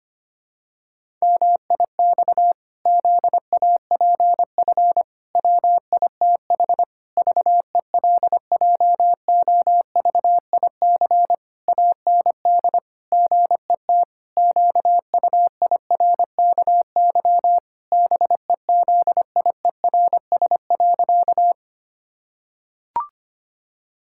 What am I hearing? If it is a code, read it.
Morse code, heard as a keyed tone, about 25 wpm, 700 Hz. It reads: MIX ZAPF WITH VELJOVIC AND GET QUIRKY BEZIERS.
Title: Pangram Challenge v2 - 25wpm